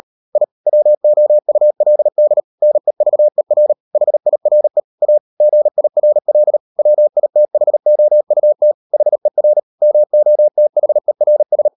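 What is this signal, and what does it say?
Morse code, heard as a keyed tone, about 38 wpm, 600 Hz. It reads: I WOULD NEVER HIRE A GIRL WITHOUT HER MOTHERS
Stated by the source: Morse code practice at 38 WPM, sending Ken Follett's novel Never in English, from CW med Gnister.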